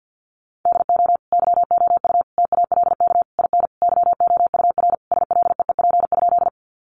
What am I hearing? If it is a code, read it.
Morse code, heard as a keyed tone, about 50 wpm, 700 Hz. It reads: DO YOU TALK IN YOUR SLEEP?
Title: Sentences from Top 500 Words 50wpm